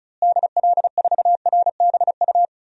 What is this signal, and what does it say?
Morse code, heard as a keyed tone, about 34 wpm, 700 Hz. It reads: DL4RBU
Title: DXCC-100 Call Signs - 6 Characters - Repeated 2x - 35wpm